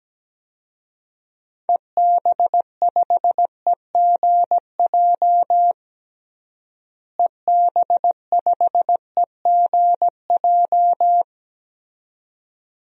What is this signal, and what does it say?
Morse code, heard as a keyed tone, about 17 wpm, 700 Hz. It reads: EB5EGJ EB5EGJ
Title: DXCC-100 Call Signs - 6 Characters - Repeated 2x - 17wpm